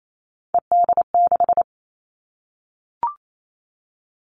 Morse code, heard as a keyed tone, about 28 wpm, 700 Hz. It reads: ED6
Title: Callsign Prefixes for Top 100 DXCC - 28wpm